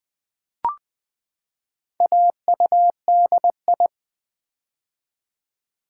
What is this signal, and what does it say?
Morse code, heard as a keyed tone, about 20 wpm, 700 Hz. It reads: AUDI
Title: Build A Word - 4 Letters At A Time - 8-Letter Words - 20wpm